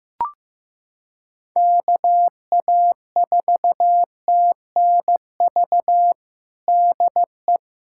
Morse code, heard as a keyed tone, about 15 wpm, 700 Hz. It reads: KA4TNV DE
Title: USA Callsign DE Callsign - 6 Characters - 15wpm